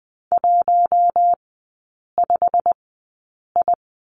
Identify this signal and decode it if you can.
Morse code, heard as a keyed tone, about 20 wpm, spaced as for 8 wpm, 690 Hz. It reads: 15I
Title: 093 Lesson 16e - Letter c - Three Characters 20wpm